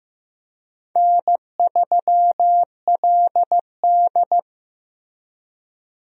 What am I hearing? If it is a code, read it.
Morse code, heard as a keyed tone, about 15 wpm, 700 Hz. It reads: N3LD
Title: USA Call Signs - 4 Characters - Repeated 2x - 15wpm